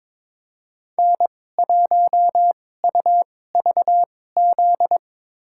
Morse code, heard as a keyed tone, about 22 wpm, spaced as for 16 wpm, 700 Hz. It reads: N1UVZ
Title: USA Call Signs - 2x Character Spacing - 22wpm